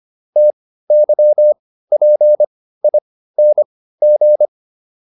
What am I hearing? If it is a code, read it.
Morse code, heard as a keyed tone, about 25 wpm, spaced as for 15 wpm, 600 Hz. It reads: TYPING